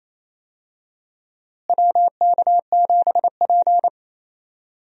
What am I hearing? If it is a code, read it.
Morse code, heard as a keyed tone, about 28 wpm, 700 Hz. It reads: WK7P
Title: USA Call Signs - 4 Characters - Repeated 2x - 28wpm